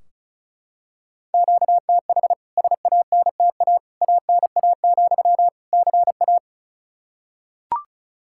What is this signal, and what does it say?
Morse code, heard as a keyed tone, about 36 wpm, 700 Hz. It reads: QTH SANTA ANA, CA
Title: K1USN SST QSO Practice 35wpm